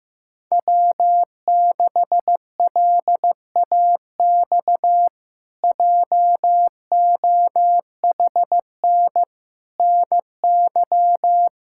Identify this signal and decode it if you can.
Morse code, heard as a keyed tone, about 15 wpm, 700 Hz. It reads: W6LAX JOHN NY
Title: K1USN SST QSO Practice - 15wpm